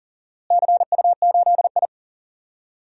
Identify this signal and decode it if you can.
Morse code, heard as a keyed tone, about 40 wpm, 700 Hz. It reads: CU8I
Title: DXCC-100 Call Signs - 4 Characters - Repeated 3x - 40wpm